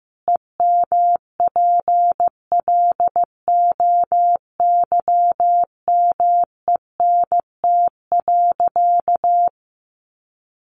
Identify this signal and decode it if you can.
Morse code, heard as a keyed tone, about 15 wpm, 700 Hz. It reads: EMPLOYMENT.
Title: Sentences from Top 1500 Words 15wpm